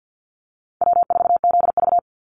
Morse code, heard as a keyed tone, about 49 wpm, 700 Hz. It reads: W4ZV
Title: NA SOTA Chaser Callsigns 50wpm